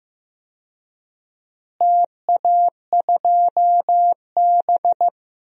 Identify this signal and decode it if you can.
Morse code, heard as a keyed tone, about 15 wpm, 700 Hz. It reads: TA2B
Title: DXCC-100 Call Signs - 4 Characters - Repeated 2x - 15wpm